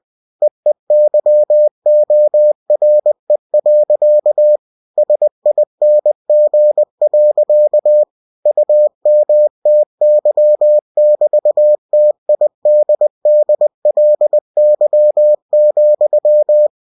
Morse code, heard as a keyed tone, about 20 wpm, 600 Hz. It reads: EEYORE. SING. UMTY<BT>TIDDLY,
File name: cw-books / winnie-the-pooh / winnie-20wpm-page-0046